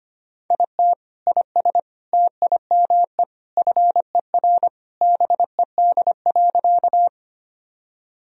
Morse code, heard as a keyed tone, about 25 wpm, 700 Hz. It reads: IT IS TIME FER BED.
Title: Sentences from Top 500 Words and Abbreviations 25wpm